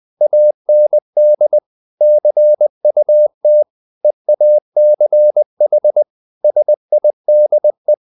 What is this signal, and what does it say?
Morse code, heard as a keyed tone, about 20 wpm, 600 Hz. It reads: AND CUT EACH SIDE
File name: the old man and the sea part 2 20wpm